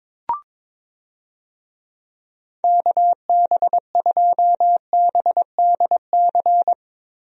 Morse code, heard as a keyed tone, about 22 wpm, 700 Hz. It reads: KB2BDC
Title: USA Call Signs - 6 Characters - Repeated 2x - 22wpm